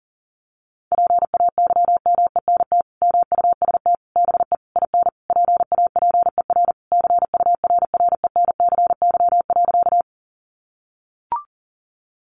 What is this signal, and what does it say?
Morse code, heard as a keyed tone, about 40 wpm, 700 Hz. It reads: PAYMENT MUST BE IN PAPER CURRENCY.
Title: Sentences from Top 1500 Words 40wpm